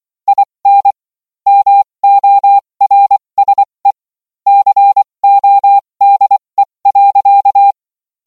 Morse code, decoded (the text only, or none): IN MORSE CODE.